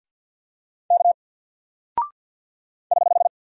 Morse code, heard as a keyed tone, about 48 wpm, 700 Hz. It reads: K <HH>
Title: Single Character (Letter:Number:Punctuation:Prosign) 50wpm